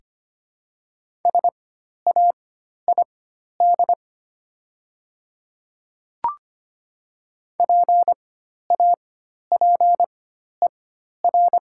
Morse code, heard as a keyed tone, about 25 wpm, spaced as for 12 wpm, 700 Hz. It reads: SAID PAPER
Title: 4x Character Spacing - Top 300 Words - 25wpm